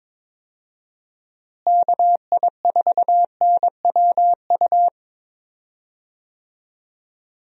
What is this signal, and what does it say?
Morse code, heard as a keyed tone, about 21 wpm, 700 Hz. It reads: KI4NWU
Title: USA Call Signs - 6 Characters - Repeated 3x - 22wpm